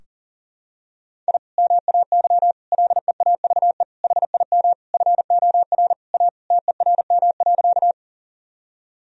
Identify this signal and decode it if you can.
Morse code, heard as a keyed tone, about 40 wpm, 700 Hz. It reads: I MAY LEAVE HIM FOR A TERM.
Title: Sentences from Top 1000 Words - Review Part 4 - 40wpm